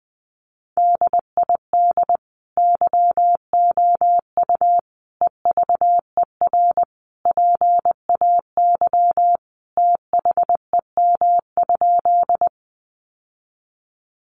Morse code, heard as a keyed tone, about 20 wpm, 700 Hz. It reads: DID YOU EVER PAY THEM?
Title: Sentences from Top 700 Words 20wpm